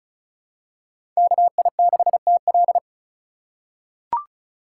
Morse code, heard as a keyed tone, about 34 wpm, 700 Hz. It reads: KI6TL
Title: USA Call Signs - 5 Characters - 3x Character Spacing - 35wpm